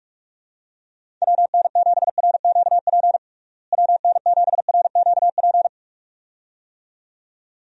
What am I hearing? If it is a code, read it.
Morse code, heard as a keyed tone, about 46 wpm, 700 Hz. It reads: WN7RQP WN7RQP
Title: USA Call Signs - 6 Characters - Repeated 2x - 45wpm